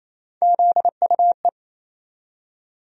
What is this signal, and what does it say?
Morse code, heard as a keyed tone, about 28 wpm, 700 Hz. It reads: ZUE
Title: USA Call Sign Postfixes - 3 Letters 28wpm